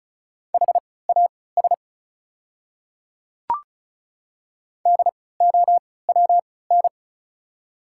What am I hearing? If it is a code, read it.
Morse code, heard as a keyed tone, about 35 wpm, spaced as for 20 wpm, 700 Hz. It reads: HAS DOWN
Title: 3x Character Spacing - Top 100 Words - 35wpm